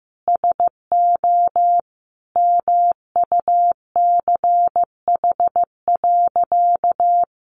Morse code, heard as a keyed tone, about 15 wpm, 700 Hz. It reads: SO MUCH.
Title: Additional Easy Phrases 15wpm